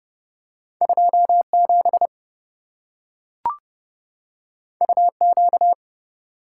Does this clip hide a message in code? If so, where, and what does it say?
Yes: Morse code from 0.8 to 5.7 s, heard as a keyed tone, about 27 wpm, 700 Hz. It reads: 27 UQ